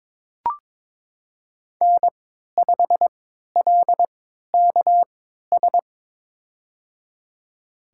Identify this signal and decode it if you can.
Morse code, heard as a keyed tone, about 22 wpm, spaced as for 13 wpm, 700 Hz. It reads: N5LKS
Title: USA Call Signs - 5 Characters - 3x Character Spacing - 22wpm